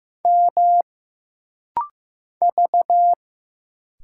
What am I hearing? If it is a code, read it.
Morse code, heard as a keyed tone, about 15 wpm, 700 Hz. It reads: M V
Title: Single Letter-Number Alt - Rapid-Fire 15wpm